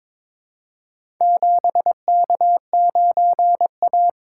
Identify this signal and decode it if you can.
Morse code, heard as a keyed tone, about 22 wpm, 700 Hz. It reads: 7K9A
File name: practice / 22wpm / Four Letter-Number Combinations - 22wpm